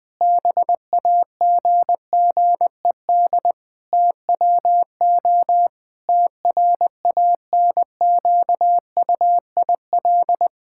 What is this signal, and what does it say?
Morse code, heard as a keyed tone, about 20 wpm, 700 Hz. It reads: BAGGED TWO TRANQUIL
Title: Pangram Challenge v2 - 20wpm